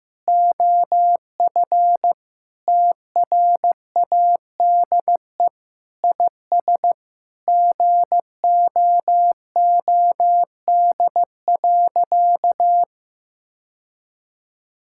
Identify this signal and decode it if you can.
Morse code, heard as a keyed tone, about 16 wpm, 700 Hz. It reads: OF TRADE IS GOOD.
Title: Sentences from Top 1000 Words - Review Part 6 - 15wpm